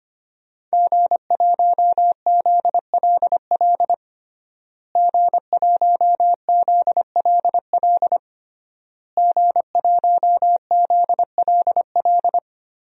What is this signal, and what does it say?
Morse code, heard as a keyed tone, about 25 wpm, 700 Hz. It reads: G1ZLL G1ZLL G1ZLL
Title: DXCC-100 Call Signs - 5 Characters - Repeated 3x - 25wpm